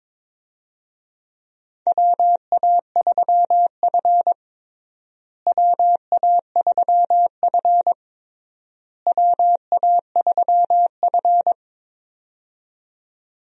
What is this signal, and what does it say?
Morse code, heard as a keyed tone, about 22 wpm, 700 Hz. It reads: WA3F WA3F WA3F
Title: USA Call Signs - 4 Characters - Repeated 3x - 22wpm